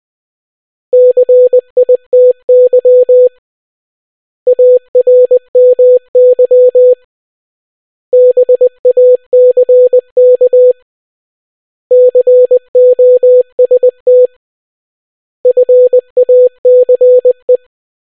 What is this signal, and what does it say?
Morse code, heard as a keyed tone, about 20 wpm, 500 Hz. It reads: CITY ARMY BACK COST FACE